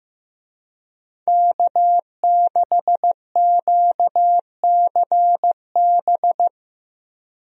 Morse code, heard as a keyed tone, about 15 wpm, 700 Hz. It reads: K6QCB
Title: USA Call Signs - 5 Characters - Repeated 2x - 15wpm